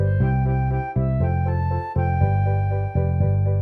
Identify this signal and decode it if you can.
There is no code.